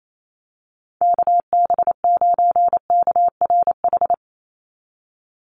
Morse code, heard as a keyed tone, about 28 wpm, 700 Hz. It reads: KB9KRH
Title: USA Call Signs 28wpm